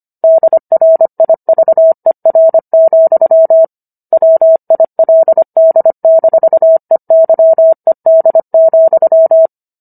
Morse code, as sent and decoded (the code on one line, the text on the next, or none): -.. .-. .. ...- . .-. --..-- / .-- .. .-.. -.. -....- . -.-- . -.. --..--
DRIVER, WILD-EYED,